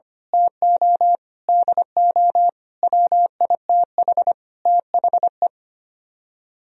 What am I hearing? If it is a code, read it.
Morse code, heard as a keyed tone, about 24 wpm, 700 Hz. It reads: TO DO WITH THE
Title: Speed-Racing - Sets of 5 Words 25wpm